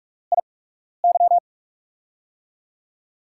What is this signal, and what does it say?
Morse code, heard as a keyed tone, about 44 wpm, spaced as for 12 wpm, 700 Hz. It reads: IY
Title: USA Call Signs - 6 Characters - 8x Character Spacing - 45wpm